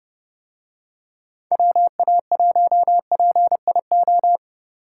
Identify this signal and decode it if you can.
Morse code, heard as a keyed tone, about 31 wpm, 700 Hz. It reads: WA1PIO